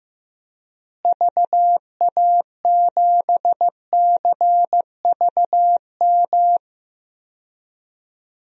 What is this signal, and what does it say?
Morse code, heard as a keyed tone, about 15 wpm, 700 Hz. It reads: VA7CVM
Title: DXCC-100 Call Signs - 6 Characters - Repeated 2x - 15wpm